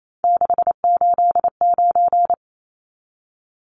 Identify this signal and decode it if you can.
Morse code, heard as a keyed tone, about 28 wpm, 700 Hz. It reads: 689
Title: Numbers - 3-Digit 28wpm